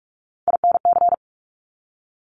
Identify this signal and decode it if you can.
Morse code, heard as a keyed tone, about 46 wpm, 700 Hz. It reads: INC
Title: Top Three Letter Combinations in Words 45wpm